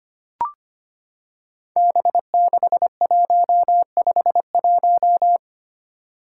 Morse code, heard as a keyed tone, about 25 wpm, 700 Hz. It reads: B6151
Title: Five Challenging Letter-Number Combinations 25wpm